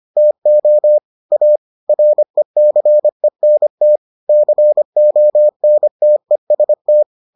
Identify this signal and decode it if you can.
Morse code, heard as a keyed tone, about 25 wpm, 600 Hz. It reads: TO A RECENT CONTEST